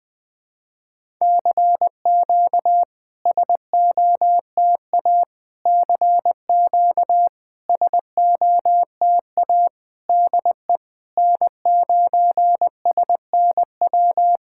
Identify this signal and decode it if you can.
Morse code, heard as a keyed tone, about 20 wpm, 700 Hz. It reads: CQ SOTA CQ SOTA DE N9SNW
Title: SOTA QSO Practice 20wpm